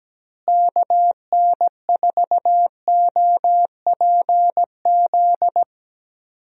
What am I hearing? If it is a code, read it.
Morse code, heard as a keyed tone, about 17 wpm, 700 Hz. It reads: KN4OPZ
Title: USA Call Signs - 6 Characters - Repeated 2x - 17wpm